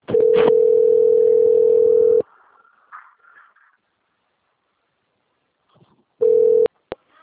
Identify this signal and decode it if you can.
Ringback tone (North America), 440 + 480 Hz.
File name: out- - - - - .20367